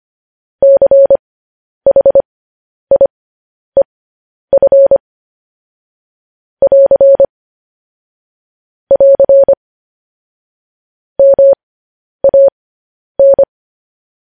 Morse code, heard as a keyed tone, about 25 wpm, spaced as for 10 wpm, 550 Hz. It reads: CHIEF <AR> <AR> MAN